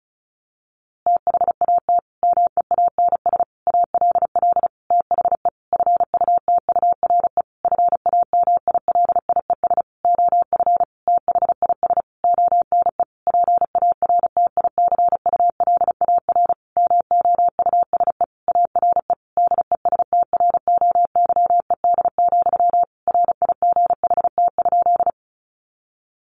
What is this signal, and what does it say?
Morse code, heard as a keyed tone, about 34 wpm, 700 Hz. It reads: THAT MEANS ALL THE FUTURE FAMILIES OF THIS ONE PARTICULAR MOUSE ARE DESTROYED, RIGHT?
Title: A Sound Of Thunder 35wpm